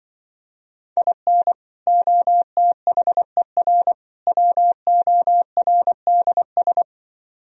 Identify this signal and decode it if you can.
Morse code, heard as a keyed tone, about 24 wpm, 700 Hz. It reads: IN OTHER WORDS
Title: Speed-Racing - Sets of 3 Words 20wpm